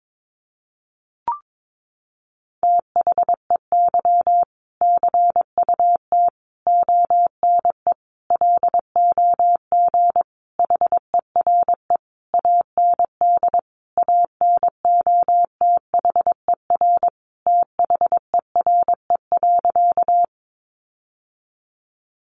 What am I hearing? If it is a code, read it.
Morse code, heard as a keyed tone, about 22 wpm, 700 Hz. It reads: THEY CUT ONE LOG HERE AND ANOTHER THERE.
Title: Sentences from Top 1400 Words 22wpm